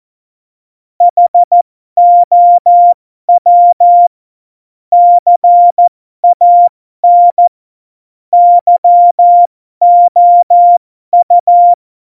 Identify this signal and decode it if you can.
Morse code, heard as a keyed tone, about 14 wpm, spaced as for 12 wpm, 700 Hz. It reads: HOW CAN YOU
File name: Morsekurs / Witze